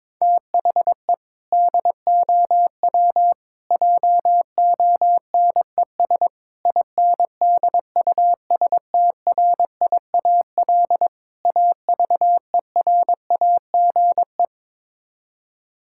Morse code, heard as a keyed tone, about 22 wpm, 700 Hz. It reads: THE DOW JONES INDUSTRIAL AVERAGE
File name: practice / 22wpm / Build A Phrase - 5-Word Phrases - 22wpm